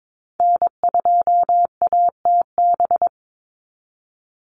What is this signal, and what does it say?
Morse code, heard as a keyed tone, about 22 wpm, 700 Hz. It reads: N2ATB
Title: Call Signs - 22wpm